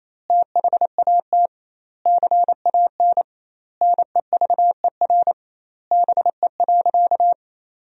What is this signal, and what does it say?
Morse code, heard as a keyed tone, about 28 wpm, 700 Hz. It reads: THAT CAN NEVER BE.
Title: 2x Word-Spacing - Sentences from Top 300 Words - 28wpm